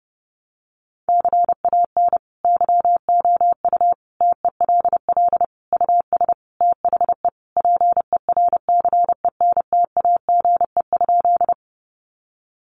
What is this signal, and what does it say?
Morse code, heard as a keyed tone, about 30 wpm, 700 Hz. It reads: CAN YOU TELL US THE PERCENTAGE?
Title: Sentences from Top 1300 Words 30wpm